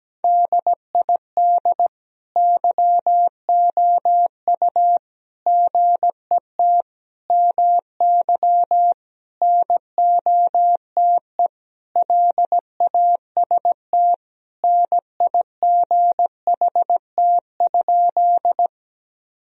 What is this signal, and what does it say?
Morse code, heard as a keyed tone, about 17 wpm, 700 Hz. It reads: DID YOU GET MY NOTE LAST NIGHT?
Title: Sentences from Top 1000 Words - Review Part 2 - 17wpm